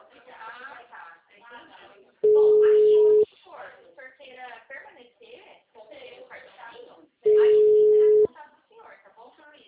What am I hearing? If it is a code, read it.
Ringback tone (Europe, 425 Hz), with speech.